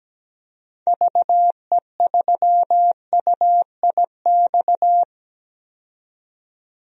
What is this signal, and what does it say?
Morse code, heard as a keyed tone, about 17 wpm, 700 Hz. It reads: VE3UIX
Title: DXCC-100 Call Signs - 6 Characters - 17wpm